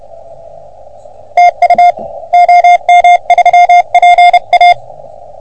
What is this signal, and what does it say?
Morse code, heard as a keyed tone, about 31 wpm, 710 Hz. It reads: TU OM3PA
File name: OM3PA PILUP CUC 388